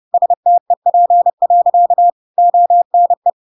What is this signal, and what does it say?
Morse code, heard as a keyed tone, about 30 wpm, 700 Hz. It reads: STEP. ONE
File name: Pinocchio / Pinocchio ch3 30 30 ik1hge